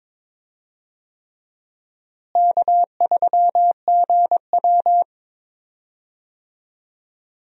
Morse code, heard as a keyed tone, about 22 wpm, 700 Hz. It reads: K3GW
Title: USA Call Signs - 4 Characters - Repeated 3x - 22wpm